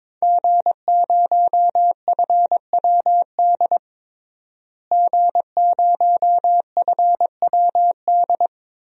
Morse code, heard as a keyed tone, about 22 wpm, 700 Hz. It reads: G0FWD G0FWD
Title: DXCC-100 Call Signs - 5 Characters - Repeated 2x - 22wpm